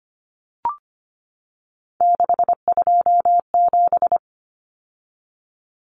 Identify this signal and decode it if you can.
Morse code, heard as a keyed tone, about 25 wpm, 700 Hz. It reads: 627
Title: Numbers - 3-Digit 25wpm